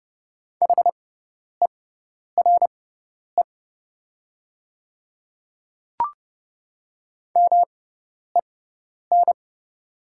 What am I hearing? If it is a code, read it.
Morse code, heard as a keyed tone, about 30 wpm, spaced as for 10 wpm, 690 Hz. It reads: HERE MEN
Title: 6x Character Spacing - Top 200 Words - 30wpm